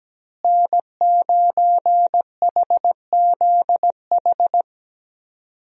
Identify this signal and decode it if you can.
Morse code, heard as a keyed tone, about 17 wpm, 700 Hz. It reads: N9HZH
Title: USA Call Signs - 5 Characters - Repeated 3x - 17wpm